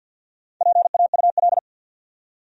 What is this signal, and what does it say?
Morse code, heard as a keyed tone, about 53 wpm, 700 Hz. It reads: PAUL
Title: USA Common Names - Max 4 Letters - 50wpm